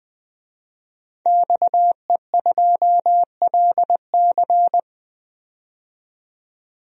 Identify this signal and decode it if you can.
Morse code, heard as a keyed tone, about 20 wpm, 700 Hz. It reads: XE2LC
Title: DXCC-100 Call Signs - 5 Characters - 20wpm